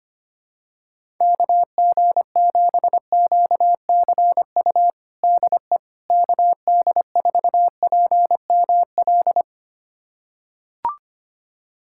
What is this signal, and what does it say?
Morse code, heard as a keyed tone, about 25 wpm, 700 Hz. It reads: KG7QCU DE KD4PML
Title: USA Callsign DE Callsign - 6 Characters - 25wpm